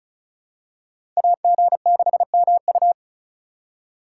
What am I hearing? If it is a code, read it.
Morse code, heard as a keyed tone, about 36 wpm, 700 Hz. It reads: AG6MU
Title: USA Call Signs - 5 Characters - 3x Character Spacing - 35wpm